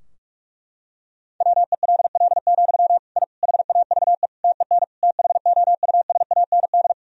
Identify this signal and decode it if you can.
Morse code, heard as a keyed tone, about 46 wpm, 700 Hz. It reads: WELL, I HAVE TEN THOUSAND
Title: Sentences from Top 1000 Words - Review Part 3 - 45wpm